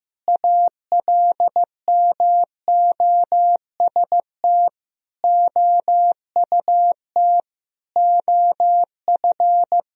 Morse code, heard as a keyed tone, about 15 wpm, 700 Hz. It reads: ALMOST OUT OF
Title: Sentences from Top 1000 Words - Review Part 4 - 15wpm